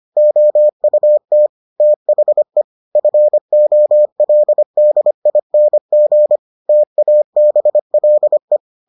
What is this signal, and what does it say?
Morse code, heard as a keyed tone, about 25 wpm, 600 Hz. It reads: OUT THE FOLDING TABLE